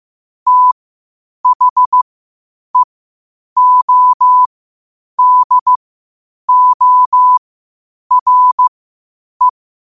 Morse code, heard as a keyed tone, about 15 wpm, spaced as for 8 wpm, 1000 Hz. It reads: THEODORE
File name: qotdfiles / 8wpm